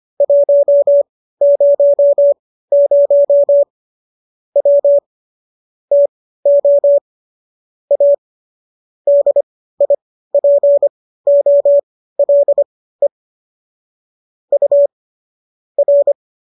Morse code, heard as a keyed tone, about 25 wpm, spaced as for 15 wpm, 570 Hz. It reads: 100 W TO A DIPOLE U R
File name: qso211 15